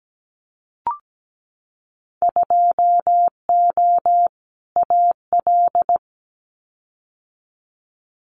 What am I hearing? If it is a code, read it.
Morse code, heard as a keyed tone, about 17 wpm, 700 Hz. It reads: 2O AL
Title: Winter Field Day Exchanges 17wpm